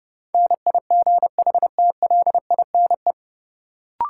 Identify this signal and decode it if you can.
Morse code, heard as a keyed tone, about 30 wpm, 700 Hz. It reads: NIGHTLINE
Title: Sets of 5 Words - New Word Drill - 30wpm